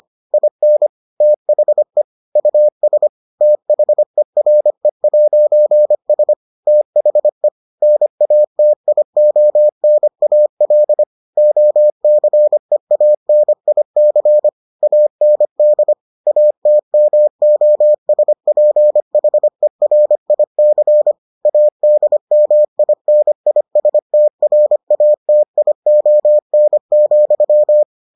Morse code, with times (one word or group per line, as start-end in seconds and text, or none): 0.3-0.9 IN
1.2-2.0 THE
2.4-3.1 US
3.4-6.3 THERE'S
6.7-7.5 THE
7.8-11.0 NATIONAL
11.4-14.5 OCEANIC
14.8-15.9 AND
16.3-21.1 ATMOSPHERIC
21.5-27.8 ADMINISTRATION,